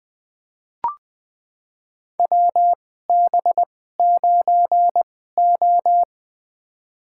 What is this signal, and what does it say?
Morse code, heard as a keyed tone, about 20 wpm, spaced as for 15 wpm, 700 Hz. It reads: WB9O
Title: USA Call Signs - 4 Characters - 2x Character Spacing - 20wpm